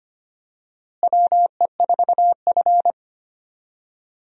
Morse code, heard as a keyed tone, about 25 wpm, 700 Hz. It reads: WE4F
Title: USA Call Signs - 4 Characters - Repeated 3x - 25wpm